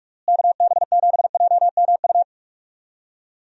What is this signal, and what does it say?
Morse code, heard as a keyed tone, about 46 wpm, 700 Hz. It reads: KB7JMU
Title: USA Call Signs - 6 Characters - Repeated 3x - 45wpm